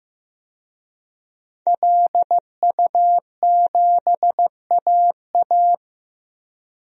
Morse code, heard as a keyed tone, about 15 wpm, 700 Hz. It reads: LU7AA
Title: DXCC-100 Call Signs - 5 Characters - Repeated 3x - 15wpm